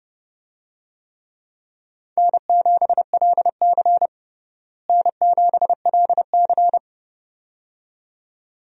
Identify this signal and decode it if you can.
Morse code, heard as a keyed tone, about 30 wpm, 700 Hz. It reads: N7LC N7LC
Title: USA Call Signs - 4 Characters - Repeated 2x - 30wpm